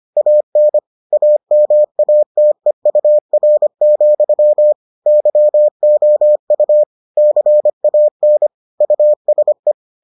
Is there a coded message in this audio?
Yes: Morse code from 0.2 to 9.7 s, heard as a keyed tone, about 25 wpm, 600 Hz. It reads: AN AMATEUR, YOU CAN USE